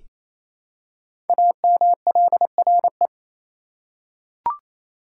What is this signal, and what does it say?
Morse code, heard as a keyed tone, about 28 wpm, 700 Hz. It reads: AMLRE
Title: Code Groups - 5 Letters - 28wpm